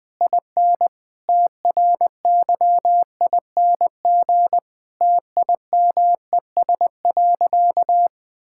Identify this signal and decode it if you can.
Morse code, heard as a keyed tone, about 20 wpm, 700 Hz. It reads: IN TRYING TIMES.
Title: Difficult Sentences with Hints - 20wpm